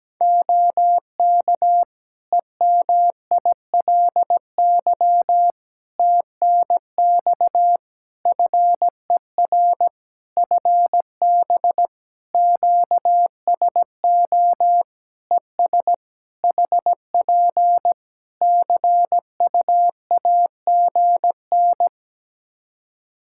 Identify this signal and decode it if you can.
Morse code, heard as a keyed tone, about 17 wpm, 700 Hz. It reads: OK EMILY TNX FER FB QSO ES HP CUAGN
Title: Practice QSO - Exchange 4 - Ending - 17wpm